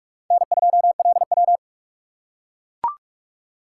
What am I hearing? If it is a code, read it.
Morse code, heard as a keyed tone, about 44 wpm, 700 Hz. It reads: N2LW